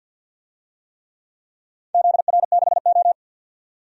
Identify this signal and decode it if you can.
Morse code, heard as a keyed tone, about 51 wpm, 700 Hz. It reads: ZR6O
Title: DXCC-100 Call Signs - 4 Characters - 50wpm